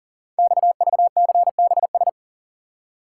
Morse code, heard as a keyed tone, about 42 wpm, 690 Hz. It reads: XVCBS
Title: Code Groups - 5 Letters - 40wpm